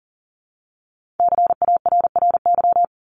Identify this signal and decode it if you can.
Morse code, heard as a keyed tone, about 40 wpm, 700 Hz. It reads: CARRY
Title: ED Suffix - Top 100 Words with Base Words 40wpm